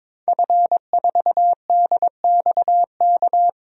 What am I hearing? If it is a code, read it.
Morse code, heard as a keyed tone, about 22 wpm, 700 Hz. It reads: F4DXK
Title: DXCC-100 Call Signs - 5 Characters - Repeated 2x - 22wpm